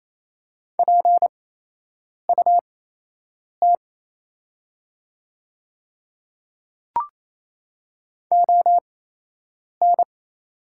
Morse code, heard as a keyed tone, about 28 wpm, spaced as for 8 wpm, 700 Hz. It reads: PUT ON
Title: 8x Character Spacing - Top 200 Words - 28wpm